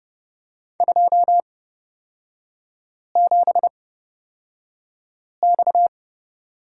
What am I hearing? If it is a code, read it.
Morse code, heard as a keyed tone, about 31 wpm, spaced as for 5 wpm, 700 Hz. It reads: 27X
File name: Lesson 38e - Letter q - Three Characters 30wpm